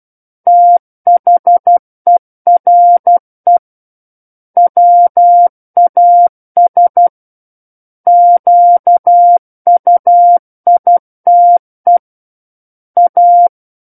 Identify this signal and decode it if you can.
Morse code, heard as a keyed tone, about 12 wpm, 700 Hz. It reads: THERE WAS QUITE A
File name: Book One / 03-Chapter 3 - 12 WPM